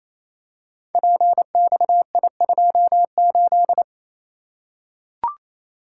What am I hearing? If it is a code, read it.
Morse code, heard as a keyed tone, about 28 wpm, 700 Hz. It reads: PXI28
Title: Code Groups - 5 Letter:Numbers - 28wpm